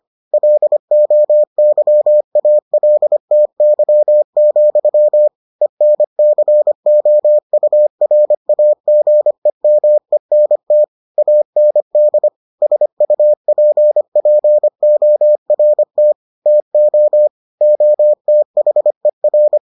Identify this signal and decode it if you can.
Morse code, heard as a keyed tone, about 25 wpm, 600 Hz. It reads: LOYALTY, ENCOURAGEMENT AND SUPPORT TO OTHER